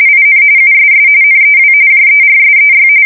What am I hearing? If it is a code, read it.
RTTY, about 45.45 baud. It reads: IN USS NEW JERSEY
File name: fsk170 / USS-New-Jersey